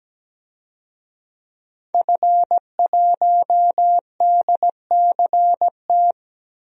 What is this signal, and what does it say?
Morse code, heard as a keyed tone, about 17 wpm, 700 Hz. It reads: F1DCT
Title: DXCC-100 Call Signs - 5 Characters - 17wpm